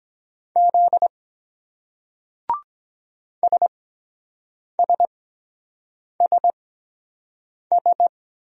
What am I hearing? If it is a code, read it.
Morse code, heard as a keyed tone, about 23 wpm, spaced as for 7 wpm, 700 Hz. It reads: Z SSSS